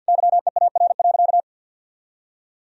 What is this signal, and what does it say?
Morse code, heard as a keyed tone, about 50 wpm, 700 Hz. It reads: YEAR.
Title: Sentences from Top 1000 Words - Review Part 3 - 50wpm